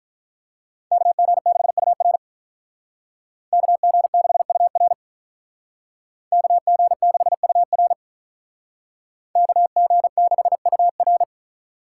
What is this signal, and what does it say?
Morse code, heard as a keyed tone, about 45 wpm, 700 Hz. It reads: KG6UR KG6UR KG6UR KG6UR